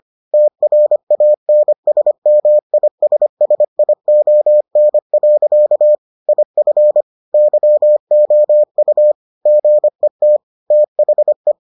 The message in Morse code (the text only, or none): TRANSMISSION. IF YOU GET THE